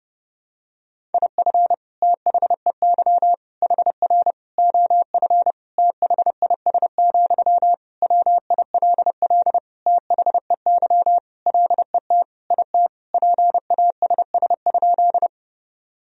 Morse code, heard as a keyed tone, about 30 wpm, 700 Hz. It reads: IF THEY HR OF THIS, WILL THEY LET IT PASS?
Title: Sentences from Top 400 Words and Abbreviations 30wpm